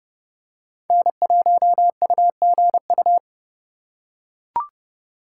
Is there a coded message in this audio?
Yes: Morse code from 0.9 to 3.2 s, heard as a keyed tone, about 29 wpm, 700 Hz. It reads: N1UGU